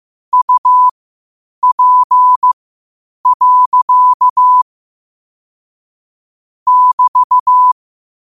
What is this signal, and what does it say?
Morse code, heard as a keyed tone, about 15 wpm, spaced as for 8 wpm, 1000 Hz. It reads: UP. <BT>
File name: qotdfiles / 8wpm